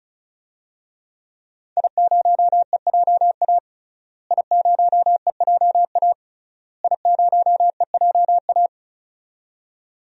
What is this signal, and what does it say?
Morse code, heard as a keyed tone, about 36 wpm, 700 Hz. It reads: I0EJA I0EJA I0EJA
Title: DXCC-100 Call Signs - 5 Characters - Repeated 3x - 35wpm